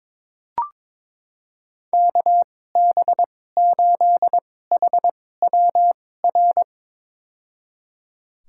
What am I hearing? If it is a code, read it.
Morse code, heard as a keyed tone, about 21 wpm, spaced as for 16 wpm, 700 Hz. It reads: KB8HWR
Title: USA Call Signs - 6 Characters - 2x Character Spacing - 22wpm